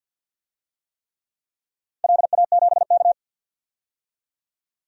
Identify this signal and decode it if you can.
Morse code, heard as a keyed tone, about 53 wpm, 700 Hz. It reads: LA7K
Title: DXCC-100 Call Signs - 4 Characters - 50wpm